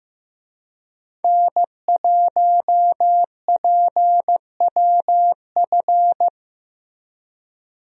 Morse code, heard as a keyed tone, about 15 wpm, 700 Hz. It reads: N1PWF
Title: USA Call Signs - 5 Characters - Repeated 3x - 15wpm